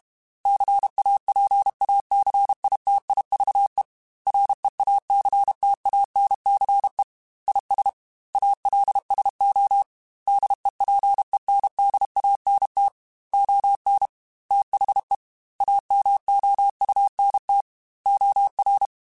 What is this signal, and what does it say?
Morse code, heard as a keyed tone, about 32 wpm, 800 Hz. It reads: CAPACITIVE REACTANCE IS ALSO DEPENDANT ON THE AMOUNT OR
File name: Technical Subjects / Technical Subjects-mp3s / XC @30